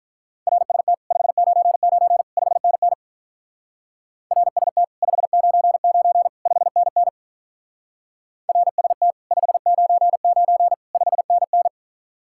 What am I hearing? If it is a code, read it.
Morse code, heard as a keyed tone, about 43 wpm, 700 Hz. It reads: RST 599 5NN RST 599 5NN RST 599 5NN
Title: Speed-Racing Common QSO Phrases - 35wpm